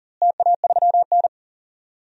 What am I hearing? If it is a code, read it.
Morse code, heard as a keyed tone, about 40 wpm, 700 Hz. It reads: TA3N
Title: DXCC-100 Call Signs - 4 Characters - Repeated 2x - 40wpm